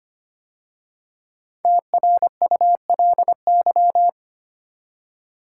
Morse code, heard as a keyed tone, about 25 wpm, 700 Hz. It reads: TRULY
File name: 5 Letter Words - First 3 Letters Given - 25wpm